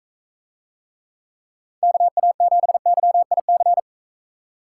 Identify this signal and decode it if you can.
Morse code, heard as a keyed tone, about 42 wpm, 700 Hz. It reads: KA7YIC